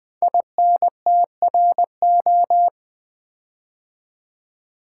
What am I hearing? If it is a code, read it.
Morse code, heard as a keyed tone, about 20 wpm, 700 Hz. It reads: INTRO
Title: Build A Word - 5 Letters At A Time - 12-Letter Words - 20wpm